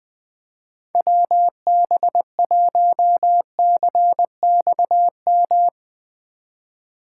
Morse code, heard as a keyed tone, about 20 wpm, 700 Hz. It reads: WB1CXM
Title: Speed Racing - USA 2x3 Call Signs 17wpm